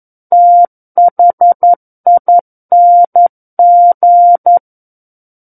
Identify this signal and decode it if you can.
Morse code, heard as a keyed tone, about 11 wpm, 700 Hz. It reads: THING
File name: 11 WPM